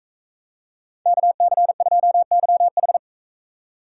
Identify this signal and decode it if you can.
Morse code, heard as a keyed tone, about 42 wpm, 700 Hz. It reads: KC2YH